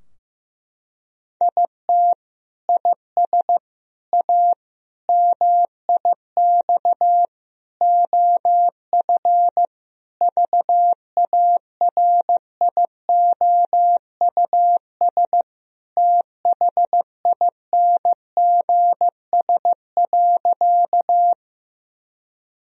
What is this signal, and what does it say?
Morse code, heard as a keyed tone, about 15 wpm, 700 Hz. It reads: IT IS A MIX OF VARIOUS THINGS.
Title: Sentences from Top 1000 Words - Review Part 4 - 15wpm